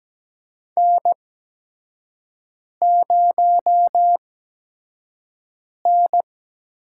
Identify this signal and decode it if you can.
Morse code, heard as a keyed tone, about 17 wpm, spaced as for 5 wpm, 700 Hz. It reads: N0N
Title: USA Call Signs - 5 Characters - 8x Character Spacing - 17wpm